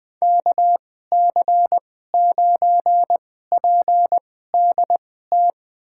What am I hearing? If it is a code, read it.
Morse code, heard as a keyed tone, about 20 wpm, spaced as for 14 wpm, 700 Hz. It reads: KC9PDT
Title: USA Call Signs - 6 Characters - 2x Character Spacing - 20wpm